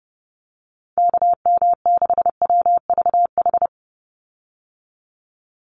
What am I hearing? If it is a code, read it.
Morse code, heard as a keyed tone, about 30 wpm, 700 Hz. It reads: KM6WVH
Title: USA Call Signs 30wpm